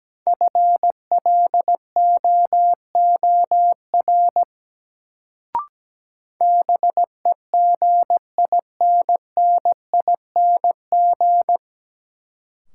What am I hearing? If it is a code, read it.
Morse code, heard as a keyed tone, about 17 wpm, 700 Hz. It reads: FLOOR BEGINNING
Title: Sets of 3 Words - New Word Drill - 17wpm